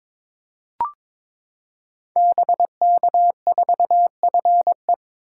Morse code, heard as a keyed tone, about 22 wpm, 700 Hz. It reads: BK4FE
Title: Code Groups - 5 Letter:Numbers - 22wpm